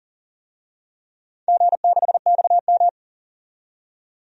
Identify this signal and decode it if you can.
Morse code, heard as a keyed tone, about 38 wpm, 700 Hz. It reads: G6XM
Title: DXCC-100 Call Signs - 4 Characters - 40wpm